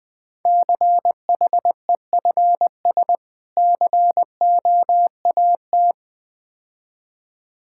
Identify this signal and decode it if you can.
Morse code, heard as a keyed tone, about 20 wpm, 700 Hz. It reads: CHEFS COAT
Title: Difficult Words - Speed-Racing with Hints - 20wpm